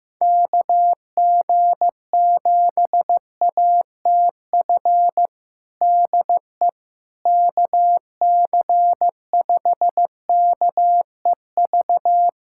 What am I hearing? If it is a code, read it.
Morse code, heard as a keyed tone, about 15 wpm, 700 Hz. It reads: KG7ATF DE KC5KEV
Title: USA Callsign DE Callsign - 6 Characters - 15wpm